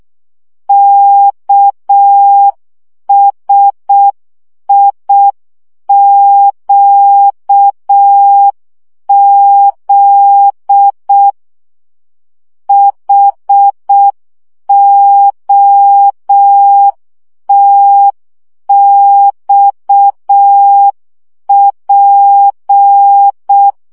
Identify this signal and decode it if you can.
Morse code, heard as a keyed tone, about 6 wpm, 800 Hz. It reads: KSIQZ HOTXP
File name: morze vizsga 2011-2 5